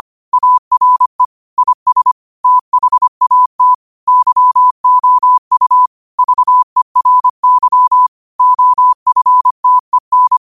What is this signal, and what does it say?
Morse code, heard as a keyed tone, about 25 wpm, 1000 Hz. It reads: ARE IS THAT YOU VERY OFTEN
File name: qotdfiles / 25wpm